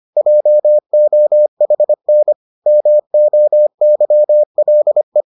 Morse code, heard as a keyed tone, about 25 wpm, 600 Hz. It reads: JOHN MOYLE